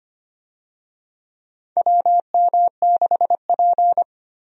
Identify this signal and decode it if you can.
Morse code, heard as a keyed tone, about 25 wpm, 700 Hz. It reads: WM6P
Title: USA Call Signs - 4 Characters - Repeated 3x - 25wpm